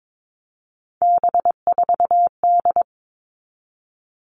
Morse code, heard as a keyed tone, about 22 wpm, 700 Hz. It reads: B4D
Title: All Cut Numbers - 3-Digit 22wpm